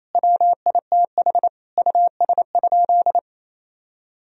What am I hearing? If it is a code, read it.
Morse code, heard as a keyed tone, about 28 wpm, 700 Hz. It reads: WITH US?
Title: Sentences from Top 1000 Words - Review Part 3 - 28wpm